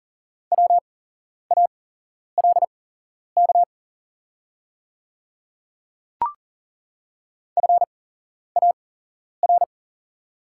Morse code, heard as a keyed tone, about 40 wpm, spaced as for 11 wpm, 690 Hz. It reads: WALK FAR
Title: 8x Character Spacing - Top 300 Words - 40wpm